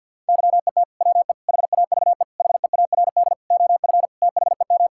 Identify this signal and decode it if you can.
Morse code, heard as a keyed tone, about 50 wpm, 690 Hz. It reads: YET WE HAVE HEARD OF THEM
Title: Sentences from Top 1000 Words - Review Part 5 - 50wpm